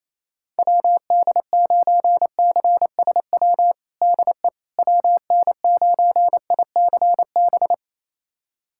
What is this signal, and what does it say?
Morse code, heard as a keyed tone, about 28 wpm, 700 Hz. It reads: WD9CSW DE WN9ICB
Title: USA Callsign DE Callsign 28wpm